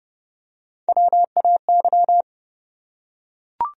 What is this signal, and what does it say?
Morse code, heard as a keyed tone, about 30 wpm, 700 Hz. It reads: WAY
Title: Top 100 Words Repeated 30wpm